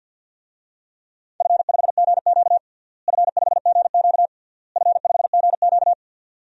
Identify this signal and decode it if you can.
Morse code, heard as a keyed tone, about 52 wpm, 700 Hz. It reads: F5GQ F5GQ F5GQ
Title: DXCC-100 Call Signs - 4 Characters - Repeated 3x - 50wpm